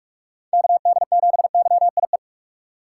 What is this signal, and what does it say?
Morse code, heard as a keyed tone, about 47 wpm, 700 Hz. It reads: KD7YIE
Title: USA Call Signs - 6 Characters - 3x Character Spacing - 45wpm